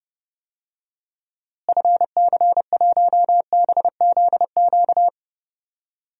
Morse code, heard as a keyed tone, about 30 wpm, 700 Hz. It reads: FC1BZQ
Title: DXCC-100 Call Signs - 6 Characters - Repeated 2x - 30wpm